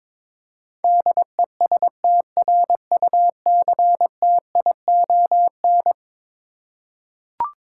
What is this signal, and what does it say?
Morse code, heard as a keyed tone, about 22 wpm, 700 Hz. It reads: DESTRUCTION
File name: practice / 22wpm / Eleven Letter Words - 22wpm